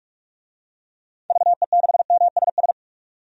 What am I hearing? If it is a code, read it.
Morse code, heard as a keyed tone, about 46 wpm, 700 Hz. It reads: VE6MSS
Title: DXCC-100 Call Signs - 6 Characters - 45wpm